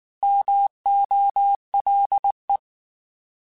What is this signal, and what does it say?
Morse code, heard as a keyed tone, about 19 wpm, 800 Hz. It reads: MOLE
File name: Alphabetical / Alphabetical mp3s / Ms @18